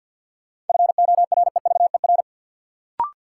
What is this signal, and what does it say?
Morse code, heard as a keyed tone, about 52 wpm, 700 Hz. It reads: FOREVER
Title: Sentences from Top 1600 Words - New Word Drill - 50wpm